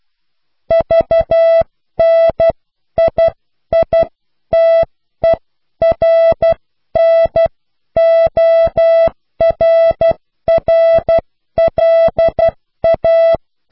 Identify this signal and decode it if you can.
Morse code, heard as a keyed tone, about 12 wpm, spaced as for 10 wpm, 660 Hz. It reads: VNIITERNORRLA
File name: cw / Pseudo woorden (15)